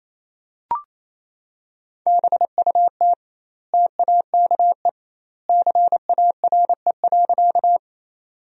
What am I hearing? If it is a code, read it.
Morse code, heard as a keyed tone, about 27 wpm, 700 Hz. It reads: BUT TAKE CARE.
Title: 2x Word-Spacing - Sentences from Top 300 Words - 28wpm